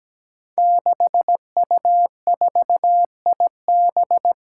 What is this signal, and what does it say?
Morse code, heard as a keyed tone, about 17 wpm, 700 Hz. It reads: 6U4IB
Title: Five Challenging Letter-Number Combinations 17wpm